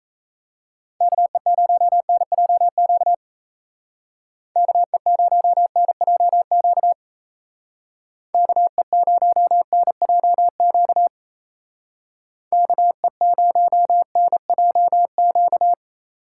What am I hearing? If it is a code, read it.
Morse code, heard as a keyed tone, about 34 wpm, 700 Hz. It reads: KE0NJQ KE0NJQ KE0NJQ KE0NJQ
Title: Speed Racing - USA 2x3 Call Signs 28wpm